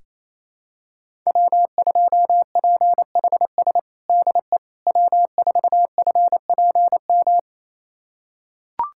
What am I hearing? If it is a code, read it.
Morse code, heard as a keyed tone, about 28 wpm, 700 Hz. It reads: W2PHS DE W4FPM
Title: USA Callsign DE Callsign - 5 Characters - 28wpm